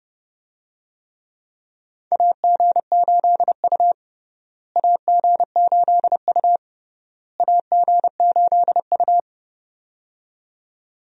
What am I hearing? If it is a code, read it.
Morse code, heard as a keyed tone, about 32 wpm, 700 Hz. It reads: AG8U AG8U AG8U
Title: USA Call Signs - 4 Characters - Repeated 3x - 30wpm